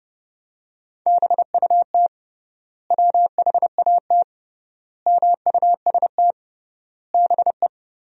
Morse code, heard as a keyed tone, about 32 wpm, 700 Hz. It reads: BUT WHAT MUST BE
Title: 3x Word-Spacing - Sentences from Top 500 Words - 30wpm